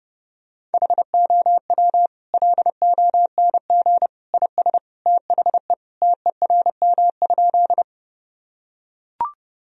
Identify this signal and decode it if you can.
Morse code, heard as a keyed tone, about 32 wpm, 700 Hz. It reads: HOW LONG IS THE TERM?
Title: Sentences from Top 1000 Words - Review Part 4 - 30wpm